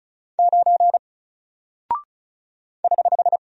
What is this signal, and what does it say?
Morse code, heard as a keyed tone, about 36 wpm, 700 Hz. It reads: 9 <HH>
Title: Single Character (Letter:Number:Punctuation:Prosign) Alt 35wpm